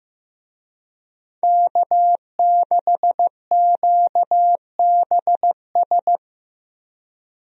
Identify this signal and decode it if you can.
Morse code, heard as a keyed tone, about 15 wpm, 700 Hz. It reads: K6QBS
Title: USA Call Signs - 5 Characters - Repeated 2x - 15wpm